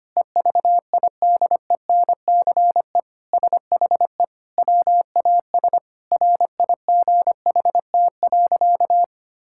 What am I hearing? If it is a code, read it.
Morse code, heard as a keyed tone, about 25 wpm, 700 Hz. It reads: EVIDENCE SHE WAS RIGHT.
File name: practice / 25wpm / Sentences with 4-Word Phrases - 25wpm